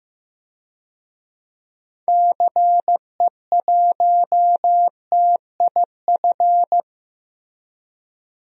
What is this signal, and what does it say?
Morse code, heard as a keyed tone, about 15 wpm, 700 Hz. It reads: CE1TIF
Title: DXCC-100 Call Signs - 6 Characters - Repeated 2x - 15wpm